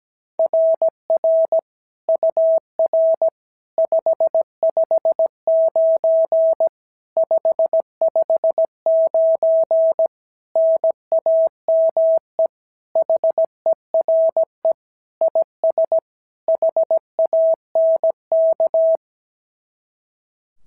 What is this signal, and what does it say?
Morse code, heard as a keyed tone, about 17 wpm, 660 Hz. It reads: RR UR 559 559 NAME HERE IS HANK
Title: SOTA QSO Practice 17wpm